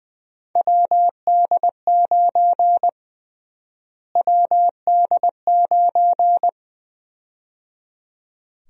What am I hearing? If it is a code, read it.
Morse code, heard as a keyed tone, about 21 wpm, 700 Hz. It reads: WD9 WD9
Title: USA Call Sign Prefixes - Repeated 2x - 20wpm